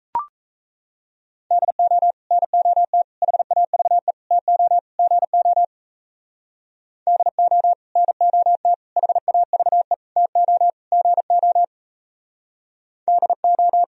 Morse code, heard as a keyed tone, about 34 wpm, 700 Hz. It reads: DO NOT HAVE TO GO DO NOT HAVE TO GO DO
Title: Speed-Racing - Sets of 5 Words 28wpm